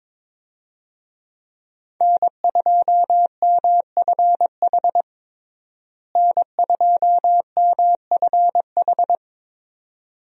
Morse code, heard as a keyed tone, about 22 wpm, 700 Hz. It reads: N2MFH N2MFH
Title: USA Call Signs - 5 Characters - Repeated 2x - 22wpm